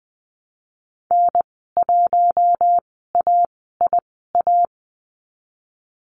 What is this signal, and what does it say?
Morse code, heard as a keyed tone, about 20 wpm, spaced as for 15 wpm, 700 Hz. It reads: N1AIA
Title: 048 Lesson 9g - Number 4 - Call Signs 20wpm